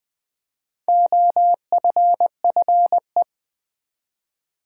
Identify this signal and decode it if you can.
Morse code, heard as a keyed tone, about 20 wpm, 700 Hz. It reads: OFFE
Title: Build A Word - 1 Letter At A Time - 5-Letter Words - 20wpm